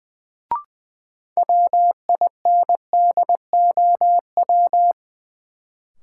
Morse code, heard as a keyed tone, about 20 wpm, 700 Hz. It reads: WINDOW
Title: Sentences from Top 1600 Words - New Word Drill - 20wpm